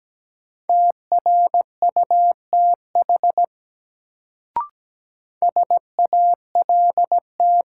Morse code, heard as a keyed tone, about 17 wpm, 700 Hz. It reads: TRUTH SALT
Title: Sentences from Top 800 Words - New Word Drill - 17wpm